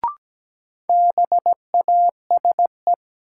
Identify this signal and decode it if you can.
Morse code, heard as a keyed tone, about 17 wpm, 700 Hz. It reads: BASE
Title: Sentences from Top 1000 Words - Review Part 1 - New Word Drill - 17wpm